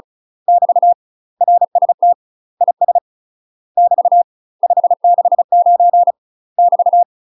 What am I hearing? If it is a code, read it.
Morse code, heard as a keyed tone, about 36 wpm, 700 Hz. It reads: <BT> RST IS <BT> 569 <BT>
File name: UK MASTER QSOs TRAINING / 35WPM / UK MASTER QSOs 0033